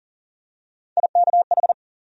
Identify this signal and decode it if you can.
Morse code, heard as a keyed tone, about 40 wpm, 700 Hz. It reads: IKH
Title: USA Call Sign Postfixes - 3 Letters 40wpm